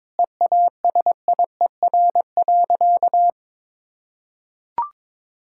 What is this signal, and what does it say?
Morse code, heard as a keyed tone, about 22 wpm, 700 Hz. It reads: EASIER.
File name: Difficult Sentences with Hints - 22wpm